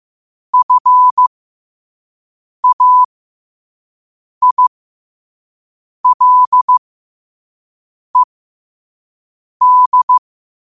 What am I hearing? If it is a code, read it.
Morse code, heard as a keyed tone, about 15 wpm, spaced as for 5 wpm, 1000 Hz. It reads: FAILED
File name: qotdfiles / 5wpm